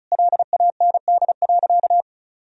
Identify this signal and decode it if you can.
Morse code, heard as a keyed tone, about 34 wpm, 700 Hz. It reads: LAND.
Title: Sentences from Top 1000 Words - Review Part 2 - 35wpm